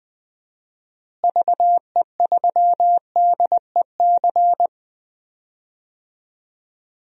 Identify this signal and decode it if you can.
Morse code, heard as a keyed tone, about 20 wpm, 700 Hz. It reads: VE3DEC
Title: DXCC-100 Call Signs - 6 Characters - Repeated 3x - 20wpm